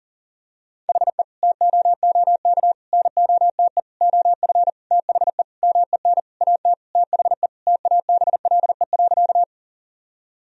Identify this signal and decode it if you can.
Morse code, heard as a keyed tone, about 37 wpm, 700 Hz. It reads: HE TOOK NOTE OF THE MEN AT THE TABLE.
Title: Sentences from Top 1000 Words - Review Part 2 - 40wpm